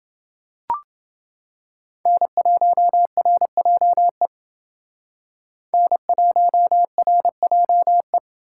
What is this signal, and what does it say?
Morse code, heard as a keyed tone, about 28 wpm, 700 Hz. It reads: N1RJE N1RJE